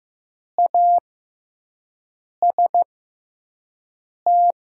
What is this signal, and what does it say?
Morse code, heard as a keyed tone, about 15 wpm, spaced as for 5 wpm, 700 Hz. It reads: AST
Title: 6x Character Spacing - Top 300 Words - 15wpm